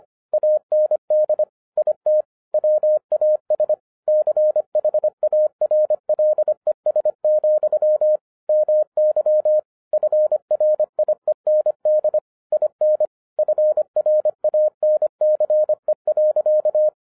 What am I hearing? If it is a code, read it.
Morse code, heard as a keyed tone, about 25 wpm, 610 Hz. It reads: AND IT WAS CHARLES, MY FRIEND IN FRANCE.
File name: 2011-07-mp3 / n34 THE 3.11 2 25wpm